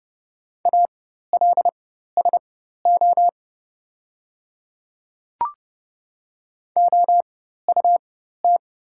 Morse code, heard as a keyed tone, about 30 wpm, spaced as for 14 wpm, 690 Hz. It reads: ALSO OUT
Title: 4x Character Spacing - Top 200 Words - 30wpm